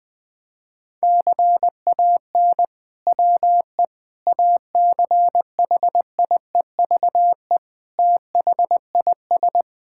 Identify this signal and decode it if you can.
Morse code, heard as a keyed tone, about 20 wpm, 700 Hz. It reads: CAN WE ACHIEVE THIS
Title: Sentences from Top 1000 Words - Review Part 5 - 20wpm